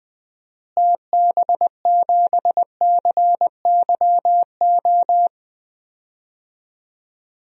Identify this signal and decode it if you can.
Morse code, heard as a keyed tone, about 20 wpm, 700 Hz. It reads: TB7CYO
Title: DXCC-100 Call Signs - 6 Characters - Repeated 3x - 20wpm